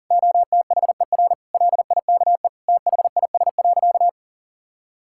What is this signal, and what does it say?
Morse code, heard as a keyed tone, about 40 wpm, 690 Hz. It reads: OTHER LIKE THIS.
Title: Sentences from Top 1000 Words - Review Part 6 - 40wpm